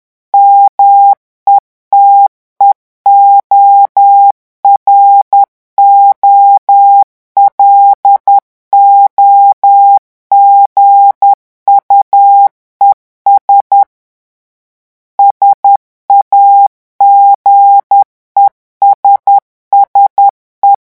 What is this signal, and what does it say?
Morse code, heard as a keyed tone, about 10 wpm, 800 Hz. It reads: METEOROLOGUES SAGESSE
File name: morse mp3 / revision francais 10 mots minute